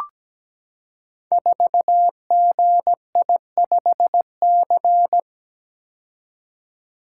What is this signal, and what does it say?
Morse code, heard as a keyed tone, about 17 wpm, 700 Hz. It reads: 4GI5C
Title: Code Groups - 5 Letter:Numbers - 17wpm